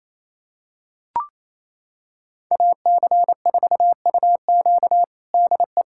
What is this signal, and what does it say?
Morse code, heard as a keyed tone, about 28 wpm, 700 Hz. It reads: AC4UQ DE
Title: USA Callsign DE Callsign - 5 Characters - 28wpm